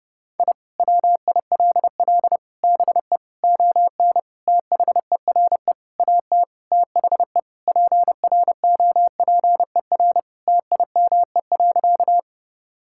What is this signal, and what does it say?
Morse code, heard as a keyed tone, about 30 wpm, 700 Hz. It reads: I WILL BE ON THERE AT THE PROPER TIME.
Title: Sentences from Top 1000 Words - Review Part 5 - 30wpm